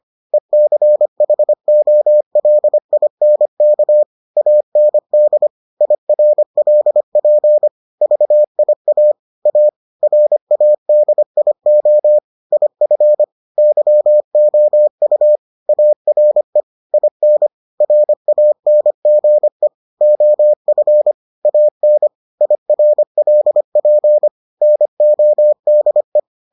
Morse code, heard as a keyed tone, about 25 wpm, 600 Hz. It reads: ECHOLINK AND IRLP VIA A RADIO IF YOU ARE IN RANGE OF AN IRLP NODE